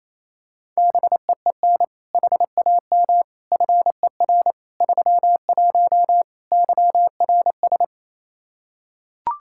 Morse code, heard as a keyed tone, about 28 wpm, 700 Hz. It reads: BEEN HAM FER 31 YRS
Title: Practice QSO - Exchange 3 - More Info - 28wpm